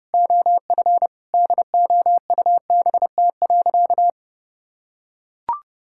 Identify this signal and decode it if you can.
Morse code, heard as a keyed tone, about 30 wpm, 700 Hz. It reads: OF DOUBT.
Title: Sentences from Top 1600 Words 30wpm